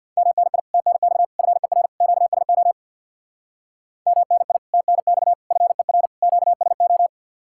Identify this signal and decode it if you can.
Morse code, heard as a keyed tone, about 61 wpm, 700 Hz. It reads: MNI TNX FER QSO MNI TNX FER QSO
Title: Speed-Racing Common QSO Phrases Encore - 50wpm